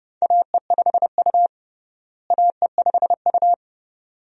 Morse code, heard as a keyed tone, about 32 wpm, 700 Hz. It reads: AE5U AE5U
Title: USA Call Signs - 4 Characters - Repeated 2x - 30wpm